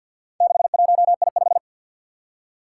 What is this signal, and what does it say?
Morse code, heard as a keyed tone, about 48 wpm, 700 Hz. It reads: 61I5
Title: Four Challenging Letter-Number Combinations 50wpm